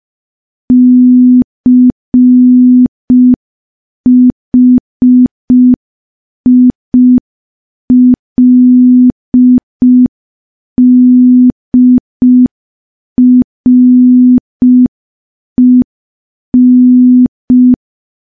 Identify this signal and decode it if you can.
Morse code, heard as a keyed tone, about 5 wpm, 250 Hz. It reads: CHILDREN
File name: morse / morse-training-05WPM